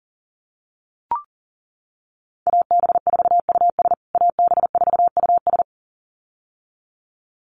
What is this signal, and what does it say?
Morse code, heard as a keyed tone, about 40 wpm, 700 Hz. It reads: AB4US AB4US
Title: USA Call Signs Repeated 2x 40wpm